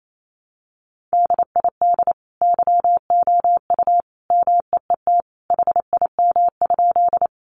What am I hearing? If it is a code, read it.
Morse code, heard as a keyed tone, about 28 wpm, 700 Hz. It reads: DID YOU MEET HIM?
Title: Sentences from Top 1100 Words 28wpm